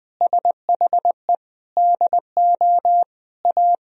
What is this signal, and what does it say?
Morse code, heard as a keyed tone, about 20 wpm, 700 Hz. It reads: SHE DO A
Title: Sentences from Top 1000 Words - Review Part 3 - 20wpm